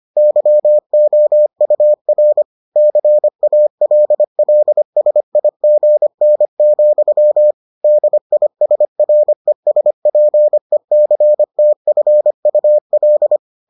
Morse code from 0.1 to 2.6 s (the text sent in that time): YOUR